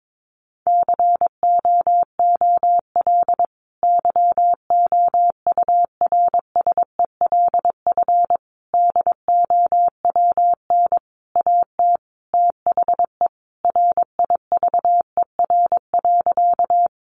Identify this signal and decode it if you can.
Morse code, heard as a keyed tone, about 22 wpm, 700 Hz. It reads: COOL YOURSELF DOWN AT THE RIVER.
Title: Sentences from Top 1200 Words 22wpm